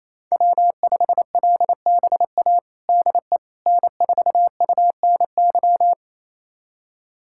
Morse code, heard as a keyed tone, about 28 wpm, 700 Hz. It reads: W5LBA DE N4UNY
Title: USA Callsign DE Callsign - 5 Characters - 28wpm